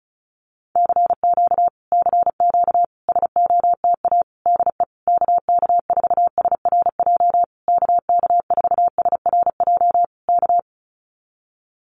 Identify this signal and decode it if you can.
Morse code, heard as a keyed tone, about 34 wpm, 700 Hz. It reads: CQ CQ SOTA DE KK4SRJ KK4SRJ K
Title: Common QSO Phrases Encore 35wpm